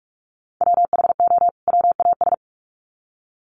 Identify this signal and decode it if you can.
Morse code, heard as a keyed tone, about 44 wpm, 700 Hz. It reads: WHO WAS
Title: Sets of 2 Words 45wpm